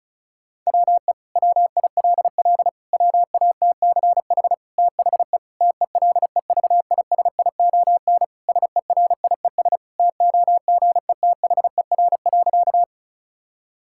Morse code, heard as a keyed tone, about 34 wpm, 700 Hz. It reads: WE WILL WATCH THE TELEVISION SERIES TOGETHER.
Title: Sentences from Top 1000 Words - Review Part 5 - 35wpm